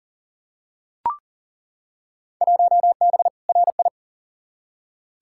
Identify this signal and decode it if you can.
Morse code, heard as a keyed tone, about 40 wpm, 700 Hz. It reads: 1B RI
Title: Field Day Exchanges Encore - 40wpm